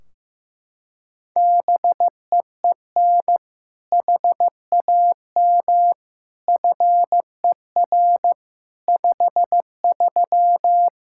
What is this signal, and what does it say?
Morse code, heard as a keyed tone, about 15 wpm, 700 Hz. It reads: BEEN HAM FER 53
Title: Practice QSO - Exchange 3 - More Info - 15wpm